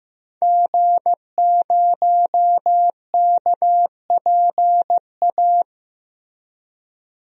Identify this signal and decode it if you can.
Morse code, heard as a keyed tone, about 15 wpm, 700 Hz. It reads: G0KPA
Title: DXCC-100 Call Signs - 5 Characters - Repeated 2x - 15wpm